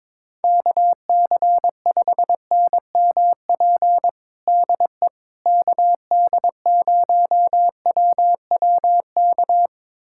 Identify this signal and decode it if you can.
Morse code, heard as a keyed tone, about 22 wpm, 700 Hz. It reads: KC5NMP DE KD0WWK
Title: USA Callsign DE Callsign 22wpm